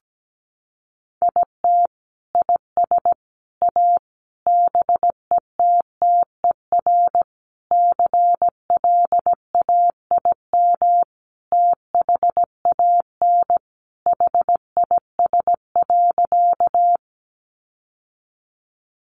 Morse code, heard as a keyed tone, about 17 wpm, 700 Hz. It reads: IT IS A BETTER CLAIM THAN HIS.
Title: Sentences from Top 1500 Words 17wpm